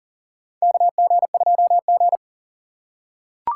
Morse code, heard as a keyed tone, about 40 wpm, 700 Hz. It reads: KG2G
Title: USA Call Signs - 4 Characters - 2x Character Spacing - 40wpm